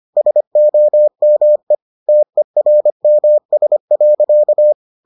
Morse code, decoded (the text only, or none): SOME TERMS.